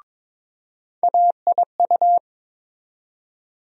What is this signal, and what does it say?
Morse code, heard as a keyed tone, about 21 wpm, 700 Hz. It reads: AIU
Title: USA Call Sign Postfixes - 3 Letters 22wpm